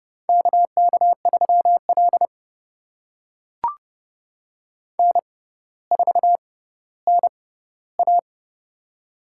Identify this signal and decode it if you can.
Morse code, heard as a keyed tone, about 32 wpm, spaced as for 16 wpm, 700 Hz. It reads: KK3L N4NA
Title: USA Call Signs - 4 Characters - 6x Character Spacing - 30wpm